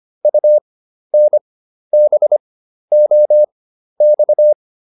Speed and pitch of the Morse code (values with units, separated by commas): 25 wpm, spaced as for 12 wpm, 600 Hz